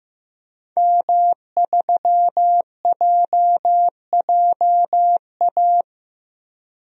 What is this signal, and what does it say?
Morse code, heard as a keyed tone, about 15 wpm, 700 Hz. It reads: M3JJA
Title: DXCC-100 Call Signs - 5 Characters - 15wpm